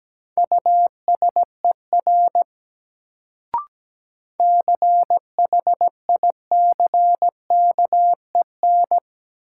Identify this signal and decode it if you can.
Morse code, heard as a keyed tone, about 17 wpm, 700 Hz. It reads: USER CHICKEN
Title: Sentences from Top 1000 Words - Review Part 5 - New Word Drill - 17wpm